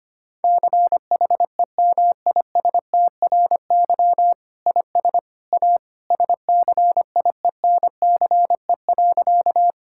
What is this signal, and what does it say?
Morse code, heard as a keyed tone, about 25 wpm, 700 Hz. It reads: CHEMISTRY IS A SCIENCE.
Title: Sentences from Top 1000 Words - Review Part 6 - 25wpm